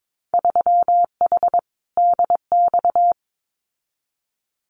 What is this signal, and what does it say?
Morse code, heard as a keyed tone, about 22 wpm, 700 Hz. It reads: 3H DX
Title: Winter Field Day Exchanges Encore 22wpm